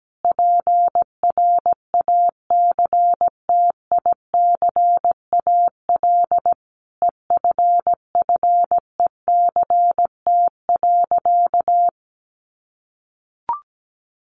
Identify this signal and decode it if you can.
Morse code, heard as a keyed tone, about 17 wpm, 700 Hz. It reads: PRACTICAL EFFECT.
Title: Sentences from Top 1400 Words 17wpm